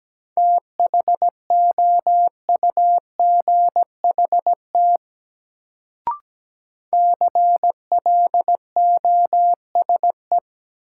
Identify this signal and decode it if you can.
Morse code, heard as a keyed tone, about 17 wpm, 700 Hz. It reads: THOUGHT CLOSE
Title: Sentences from Top 300 Words - New Word Drill - 17wpm